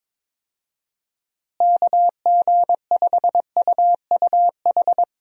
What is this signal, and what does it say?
Morse code, heard as a keyed tone, about 22 wpm, 700 Hz. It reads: KG5UUH
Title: USA Call Signs - 6 Characters - Repeated 3x - 22wpm